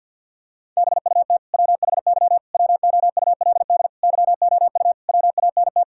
Morse code, heard as a keyed tone, about 47 wpm, 700 Hz. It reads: BUT WHY WOULD YOU WANT
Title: Sentences from Top 1000 Words - Review Part 6 - 50wpm